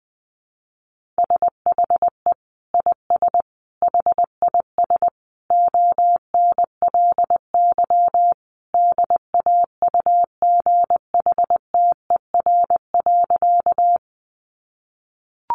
Morse code, heard as a keyed tone, about 20 wpm, 700 Hz. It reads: SHE IS HIS ONLY DAUGHTER.
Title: Sentences from Top 1500 Words 20wpm